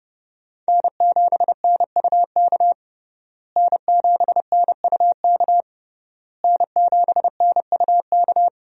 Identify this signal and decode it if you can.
Morse code, heard as a keyed tone, about 29 wpm, 700 Hz. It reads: N7NUK N7NUK N7NUK
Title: USA Call Signs - 5 Characters - Repeated 3x - 30wpm